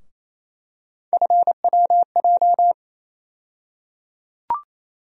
Morse code, heard as a keyed tone, about 28 wpm, 700 Hz. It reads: FWJ
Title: USA Call Sign Postfixes - 3 Letters 28wpm